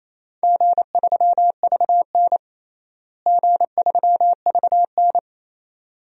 Morse code, heard as a keyed tone, about 28 wpm, 700 Hz. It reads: G3VN G3VN
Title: DXCC-100 Call Signs - 4 Characters - Repeated 2x - 28wpm